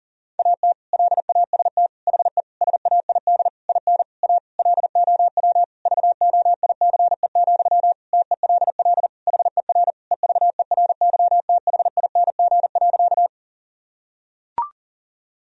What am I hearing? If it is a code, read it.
Morse code, heard as a keyed tone, about 40 wpm, 700 Hz. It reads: AT LAST HE SAID IN A LOW VOICE, TELL HER EVERYTHING.
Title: Sentences from Top 1000 Words - Review Part 1 - 40wpm